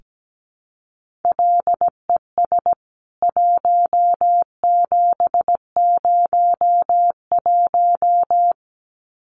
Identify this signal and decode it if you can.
Morse code, heard as a keyed tone, about 17 wpm, 700 Hz. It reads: LES 1701
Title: CWT Exchange 17wpm